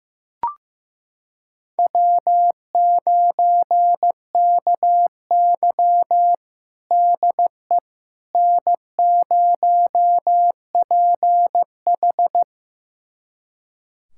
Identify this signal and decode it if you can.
Morse code, heard as a keyed tone, about 15 wpm, 700 Hz. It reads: W9KY DE N0PH
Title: USA Callsign DE Callsign - 4 Characters - 15wpm